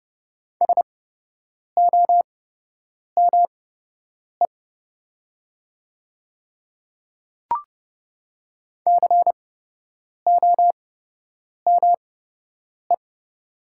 Morse code, heard as a keyed tone, about 30 wpm, spaced as for 8 wpm, 700 Hz. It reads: SOME COME
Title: 8x Character Spacing - Top 300 Words - 30wpm